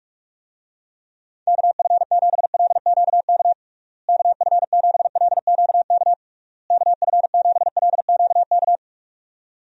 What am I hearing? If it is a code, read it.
Morse code, heard as a keyed tone, about 41 wpm, 700 Hz. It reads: KF7LQK KF7LQK KF7LQK
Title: USA Call Signs - 6 Characters - Repeated 3x - 45wpm